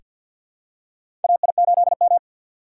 Morse code, heard as a keyed tone, about 47 wpm, 700 Hz. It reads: AI8M
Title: USA Call Signs - 4 Characters - 6x Character Spacing - 50wpm